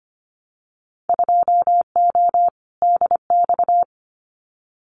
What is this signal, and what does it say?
Morse code, heard as a keyed tone, about 25 wpm, 700 Hz. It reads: 2O DX
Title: Winter Field Day Exchanges Encore 25wpm